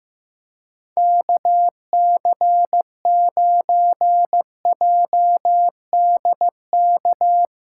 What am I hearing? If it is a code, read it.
Morse code, heard as a keyed tone, about 15 wpm, 700 Hz. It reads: KC9JDK
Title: USA Callsign DE Callsign - 6 Characters - 15wpm